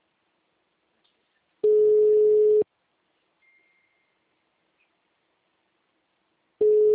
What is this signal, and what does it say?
Ringback tone (Europe, 425 Hz).
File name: out- -2006- - - .45160